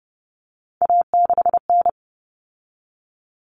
Morse code, heard as a keyed tone, about 30 wpm, 700 Hz. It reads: A6N
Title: All Cut Numbers - 3-Digit 30wpm